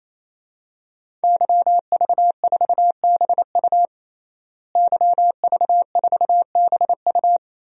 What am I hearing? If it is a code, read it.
Morse code, heard as a keyed tone, about 28 wpm, 700 Hz. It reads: YV4BU YV4BU
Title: DXCC-100 Call Signs - 5 Characters - Repeated 3x - 28wpm